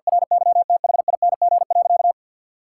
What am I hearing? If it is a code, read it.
Morse code, heard as a keyed tone, about 52 wpm, 700 Hz. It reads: RYTHING.
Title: Sentences from Top 1000 Words - Review Part 6 - 50wpm